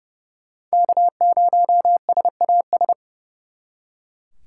Morse code, heard as a keyed tone, about 38 wpm, 700 Hz. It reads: K0SAS